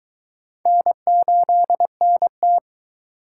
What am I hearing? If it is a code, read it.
Morse code, heard as a keyed tone, about 23 wpm, 700 Hz. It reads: N8NT